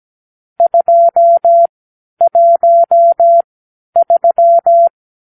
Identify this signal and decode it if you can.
Morse code, heard as a keyed tone, about 17 wpm, spaced as for 10 wpm, 680 Hz. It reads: 213
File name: k7qo / Lesson 46 (1-4)